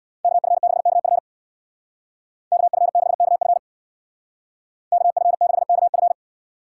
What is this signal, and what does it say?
Morse code, heard as a keyed tone, about 58 wpm, 700 Hz. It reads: KF6KF KF6KF KF6KF
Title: Speed Racing - USA 2x2 Call Signs 50wpm